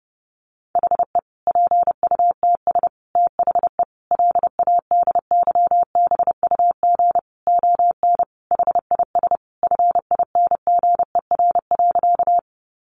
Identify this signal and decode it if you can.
Morse code, heard as a keyed tone, about 30 wpm, 700 Hz. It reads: HE PUTS THE LADYBUG ON HIS FINGER.
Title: Additional Easy Phrases 30wpm